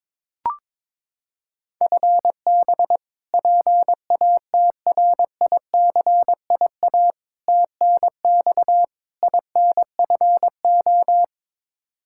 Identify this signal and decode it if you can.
Morse code, heard as a keyed tone, about 23 wpm, 700 Hz. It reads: FB PATRICIA TNX INFO
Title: Practice QSO - Exchange 3 - More Info - 22wpm